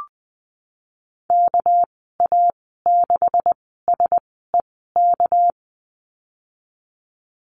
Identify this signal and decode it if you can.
Morse code, heard as a keyed tone, about 20 wpm, spaced as for 15 wpm, 700 Hz. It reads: KA6SEK
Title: 219 Lesson 35g - Number 0 - Call Signs 20wpm